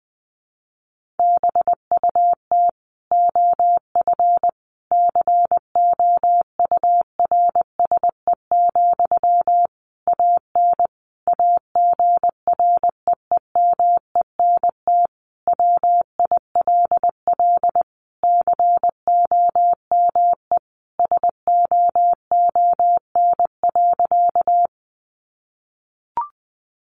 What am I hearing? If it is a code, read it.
Morse code, heard as a keyed tone, about 19 wpm, 700 Hz. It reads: BUT OF COURSE, AN AGREEMENT WILL COME SOON.
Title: Sentences from Top 1500 Words 20wpm